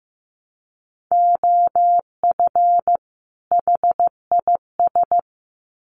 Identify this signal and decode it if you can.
Morse code, heard as a keyed tone, about 15 wpm, 700 Hz. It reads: OF HIS
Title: Sets of 2 Words 15wpm